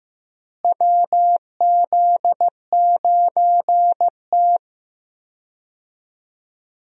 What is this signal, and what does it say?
Morse code, heard as a keyed tone, about 15 wpm, 700 Hz. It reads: WZ9T
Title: USA Call Signs - 4 Characters - Repeated 2x - 15wpm